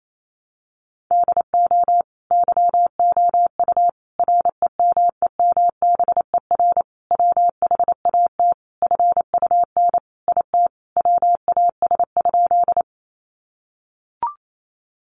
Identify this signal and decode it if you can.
Morse code, heard as a keyed tone, about 28 wpm, 700 Hz. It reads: DO YOU REMEMBER WHAT FUN IT WAS?
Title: Sentences from Top 700 Words 28wpm